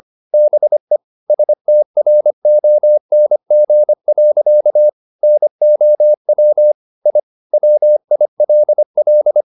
Morse code, heard as a keyed tone, about 25 wpm, 600 Hz. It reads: BE STRONG. NOW I WILL